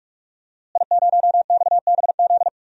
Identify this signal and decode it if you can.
Morse code, heard as a keyed tone, about 45 wpm, 700 Hz. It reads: I0XBZ
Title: DXCC-100 Call Signs - 5 Characters - Repeated 2x - 45wpm